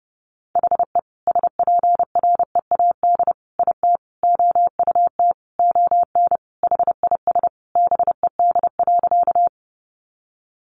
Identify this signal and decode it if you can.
Morse code, heard as a keyed tone, about 30 wpm, 700 Hz. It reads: HE SPREAD IT OUT ON HIS BED.
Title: Sentences from Top 1200 Words 30wpm